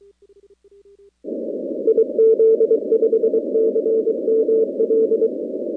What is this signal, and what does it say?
Morse code, heard as a keyed tone, about 23 wpm, 450 Hz. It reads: IZ5CML
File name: contests / 2019-cw-wpx-kt5j-k5pi- -2330 0-41